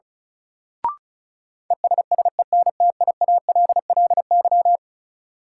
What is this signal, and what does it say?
Morse code, heard as a keyed tone, about 34 wpm, 700 Hz. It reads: ESSENTIALLY
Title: Sentences from Top 1300 Words - New Word Drill Alt - 35wpm